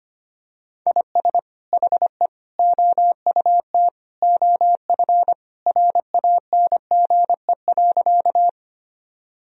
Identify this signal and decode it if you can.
Morse code, heard as a keyed tone, about 25 wpm, 700 Hz. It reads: IS HE OUT OF RANGE.
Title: Sentences from Top 1000 Words - Review Part 4 - 25wpm